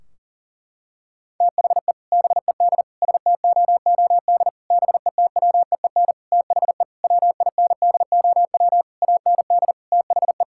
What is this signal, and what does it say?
Morse code, heard as a keyed tone, about 40 wpm, 700 Hz. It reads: THE BED STOOD BETWEEN THE WINDOW AND THE
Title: Sentences from Top 1600 Words 40wpm